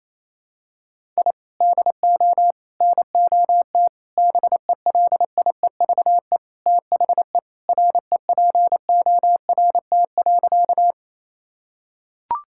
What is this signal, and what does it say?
Morse code, heard as a keyed tone, about 28 wpm, 700 Hz. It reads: I DO NOT BELIEVE THE REPORT.
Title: Sentences from Top 1000 Words - Review Part 6 - 28wpm